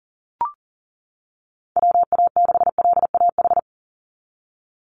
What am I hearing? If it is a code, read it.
Morse code, heard as a keyed tone, about 37 wpm, 700 Hz. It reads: WA6LAH
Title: USA Call Signs 40wpm